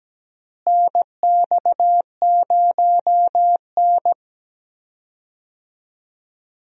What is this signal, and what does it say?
Morse code, heard as a keyed tone, about 17 wpm, 700 Hz. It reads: NX0N
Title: USA Call Signs - 4 Characters - Repeated 2x - 17wpm